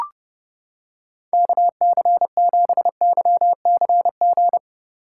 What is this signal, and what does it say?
Morse code, heard as a keyed tone, about 29 wpm, 700 Hz. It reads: KC7YCG